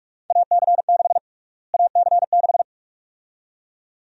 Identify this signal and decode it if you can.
Morse code, heard as a keyed tone, about 46 wpm, 700 Hz. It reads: AC6 AC6
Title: USA Call Sign Prefixes - Repeated 2x - 45wpm